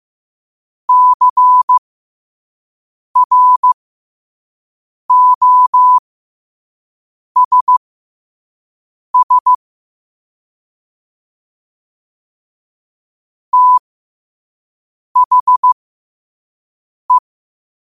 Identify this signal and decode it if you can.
Morse code, heard as a keyed tone, about 15 wpm, spaced as for 5 wpm, 1000 Hz. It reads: CROSS THE
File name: qotdfiles / 5wpm